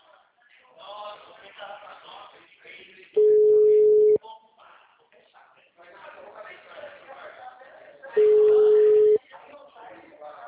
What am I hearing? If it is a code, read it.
Ringback tone (Europe, 425 Hz), with speech.
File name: out- -2004- - - .50895